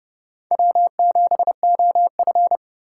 Morse code, heard as a keyed tone, about 30 wpm, 700 Hz. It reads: W7OF
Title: USA Call Signs - 4 Characters - 3x Character Spacing - 30wpm